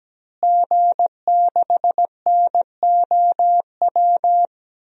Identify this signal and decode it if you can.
Morse code, heard as a keyed tone, about 17 wpm, 700 Hz. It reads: G6NOW
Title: DXCC-100 Call Signs - 5 Characters - Repeated 3x - 17wpm